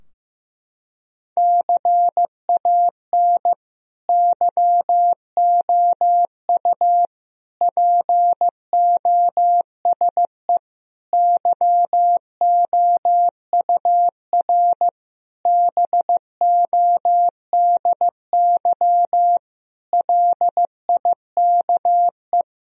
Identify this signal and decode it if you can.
Morse code, heard as a keyed tone, about 15 wpm, 700 Hz. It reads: CAN YOU POSE YOUR BODY LIKE
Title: Sentences from Top 1000 Words - Review Part 2 - 15wpm